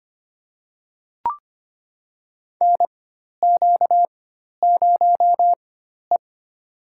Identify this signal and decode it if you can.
Morse code, heard as a keyed tone, about 25 wpm, spaced as for 12 wpm, 700 Hz. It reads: NQ0E
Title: USA Call Signs - 4 Characters - 4x Character Spacing - 25wpm